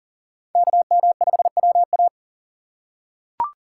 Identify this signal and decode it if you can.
Morse code, heard as a keyed tone, about 37 wpm, 700 Hz. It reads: KM5WA
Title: USA Call Signs - 5 Characters - 4x Character Spacing - 40wpm